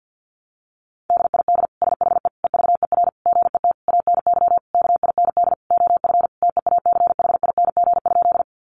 Morse code, heard as a keyed tone, about 48 wpm, 700 Hz. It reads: DID SHE EVER GET ANY KIND OF TEACHING?
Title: Sentences from Top 1100 Words 50wpm